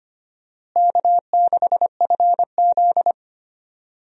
Morse code, heard as a keyed tone, about 25 wpm, 700 Hz. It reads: K6FZ
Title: USA Call Signs - 4 Characters - Repeated 2x - 25wpm